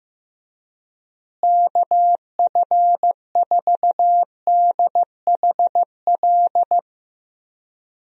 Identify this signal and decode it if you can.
Morse code, heard as a keyed tone, about 15 wpm, 700 Hz. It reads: KF4DHL
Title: USA Call Signs - 6 Characters - Repeated 2x - 15wpm